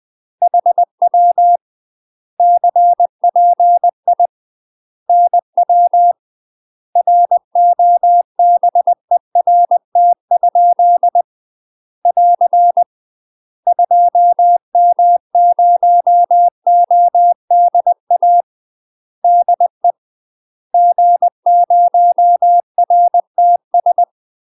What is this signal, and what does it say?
Morse code, heard as a keyed tone, about 20 wpm, 700 Hz. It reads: HW CPI NW ROBERT? <AR> 2M0ODA DE G0RTS
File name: UK MASTER QSOs TRAINING / 20WPM / UK MASTER QSOs 0041